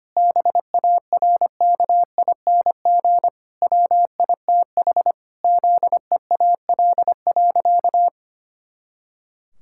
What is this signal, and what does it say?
Morse code, heard as a keyed tone, about 25 wpm, 700 Hz. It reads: BARKING WITH ZEAL.
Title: Pangram Challenge v2 - 25wpm